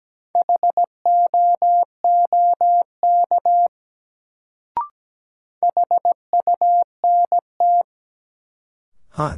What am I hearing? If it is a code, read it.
Morse code, heard as a keyed tone, about 17 wpm, 700 Hz. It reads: HOOK HUNT
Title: Sentences from Top 1600 Words - New Word Drill - 17wpm